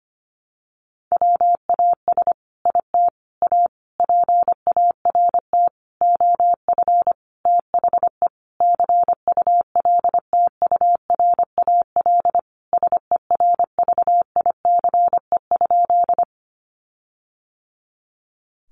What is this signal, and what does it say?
Morse code, heard as a keyed tone, about 25 wpm, 700 Hz. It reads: WAS IT A PART OF THE CULTURAL SERVICE?
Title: Sentences from Top 1500 Words 25wpm